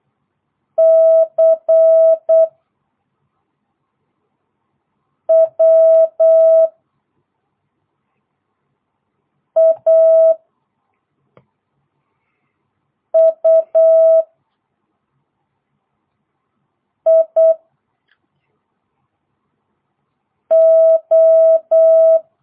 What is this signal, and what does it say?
Morse code, heard as a keyed tone, about 8 wpm, 650 Hz. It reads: C W A U I O